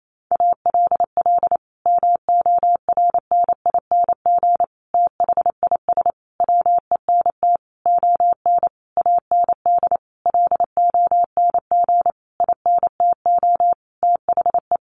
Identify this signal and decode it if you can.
Morse code, heard as a keyed tone, about 28 wpm, 700 Hz. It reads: ALL MORNING THIS WENT ON AND LONG INTO THE
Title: Sentences from Top 1500 Words 28wpm